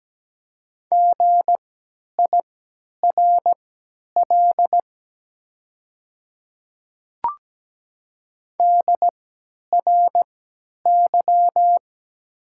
Morse code, heard as a keyed tone, about 17 wpm, spaced as for 10 wpm, 700 Hz. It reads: GIRL DRY
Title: 3x Character Spacing - Top 500 Words - 17wpm